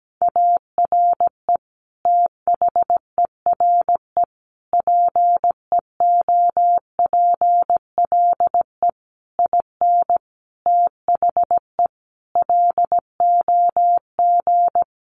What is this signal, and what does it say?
Morse code, heard as a keyed tone, about 17 wpm, 700 Hz. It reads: ARE THERE PEOPLE IN THE LOG
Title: Sentences from Top 1400 Words 17wpm